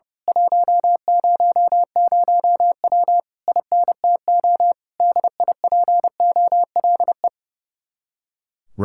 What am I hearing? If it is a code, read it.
Morse code, heard as a keyed tone, about 28 wpm, 700 Hz. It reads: 100W INTO DIPOLE
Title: Speed-Racing Common QSO Phrases Encore - 30wpm